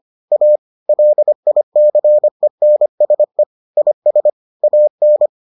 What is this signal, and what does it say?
Morse code, heard as a keyed tone, about 25 wpm, 600 Hz. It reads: A LICENSE IS AN